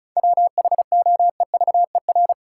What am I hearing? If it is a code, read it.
Morse code, heard as a keyed tone, about 36 wpm, 700 Hz. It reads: WHOEVER
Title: Seven Letter Words - 35wpm